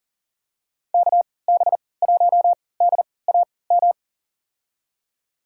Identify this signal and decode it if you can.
Morse code, heard as a keyed tone, about 40 wpm, spaced as for 23 wpm, 700 Hz. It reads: KB1DAM
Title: USA Call Signs - 6 Characters - 3x Character Spacing - 40wpm